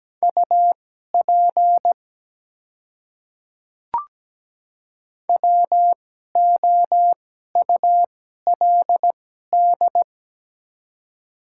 Morse code, heard as a keyed tone, about 17 wpm, spaced as for 12 wpm, 700 Hz. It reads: UP WOULD
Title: 2x Character Spacing - Top 100 Words - 17wpm